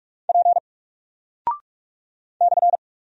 Morse code, heard as a keyed tone, about 44 wpm, 700 Hz. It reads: P /
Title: Single Character (Letter:Number:Punctuation:Prosign) Alt 45wpm